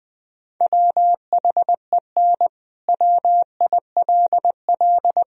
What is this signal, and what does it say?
Morse code, heard as a keyed tone, about 20 wpm, 700 Hz. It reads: WHEN WILL
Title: Sentences from Top 1000 Words - Review Part 3 - 20wpm